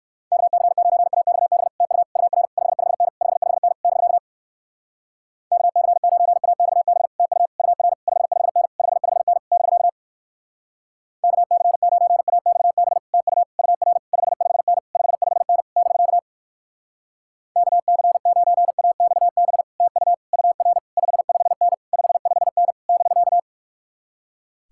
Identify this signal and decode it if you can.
Morse code, heard as a keyed tone, about 59 wpm, 700 Hz. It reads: KC9AXB TU UR 55N 55N <BK> KC9AXB TU UR 55N 55N <BK> KC9AXB TU UR 55N 55N <BK> KC9AXB TU UR 55N 55N <BK>